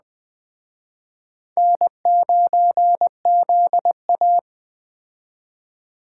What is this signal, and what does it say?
Morse code, heard as a keyed tone, about 21 wpm, 700 Hz. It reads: N9ZA